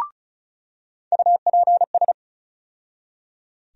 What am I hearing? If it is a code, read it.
Morse code, heard as a keyed tone, about 36 wpm, 700 Hz. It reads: UPS
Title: USA Call Sign Postfixes - 3 Letters 35wpm